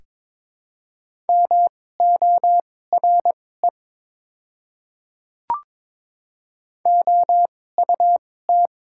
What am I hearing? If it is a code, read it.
Morse code, heard as a keyed tone, about 22 wpm, spaced as for 16 wpm, 700 Hz. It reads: MORE OUT
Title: 2x Character Spacing - Top 100 Words - 22wpm